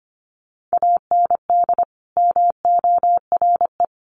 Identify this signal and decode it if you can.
Morse code, heard as a keyed tone, about 25 wpm, 700 Hz. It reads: AND MORE
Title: Sets of 2 Words 25wpm